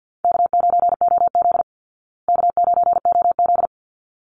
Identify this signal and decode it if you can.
Morse code, heard as a keyed tone, about 49 wpm, 700 Hz. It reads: K9OZ K9OZ
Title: Call Signs Repeated 3x - 50wpm